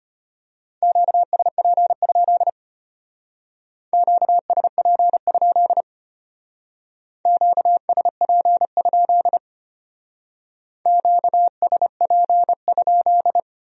Morse code, heard as a keyed tone, about 30 wpm, 700 Hz. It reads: QSP? QSP? QSP? QSP?
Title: Speed-Racing Q-Codes - 25wpm